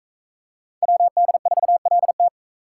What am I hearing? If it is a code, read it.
Morse code, heard as a keyed tone, about 40 wpm, 700 Hz. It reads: WD4LT